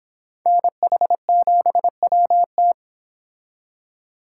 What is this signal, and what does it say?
Morse code, heard as a keyed tone, about 26 wpm, 700 Hz. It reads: NH7WT